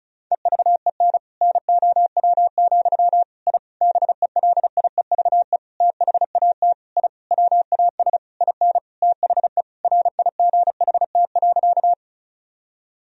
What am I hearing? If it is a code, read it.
Morse code, heard as a keyed tone, about 34 wpm, 700 Hz. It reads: EVEN NOW, I BELIEVE THAT I WAS IN THE RIGHT.
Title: Sentences from Top 1000 Words - Review Part 1 - 35wpm